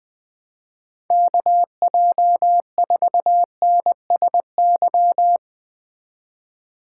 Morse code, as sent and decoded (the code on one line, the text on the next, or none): -.- .--- ....- -. ... -.--
KJ4NSY